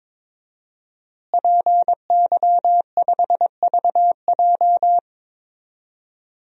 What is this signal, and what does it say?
Morse code, heard as a keyed tone, about 22 wpm, 700 Hz. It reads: PY5VJ
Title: DXCC-100 Call Signs - 5 Characters - Repeated 3x - 22wpm